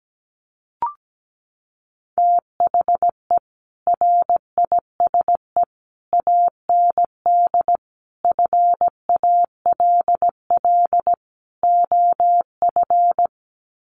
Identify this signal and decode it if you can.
Morse code, heard as a keyed tone, about 17 wpm, 700 Hz. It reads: THE RISE AND FALL OF
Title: Sets of 5 Words Encore2 17wpm